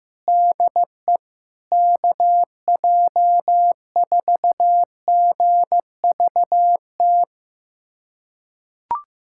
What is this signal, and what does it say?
Morse code, heard as a keyed tone, about 15 wpm, 700 Hz. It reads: DE KJ4GVT
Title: USA Callsign DE Callsign - 6 Characters - 15wpm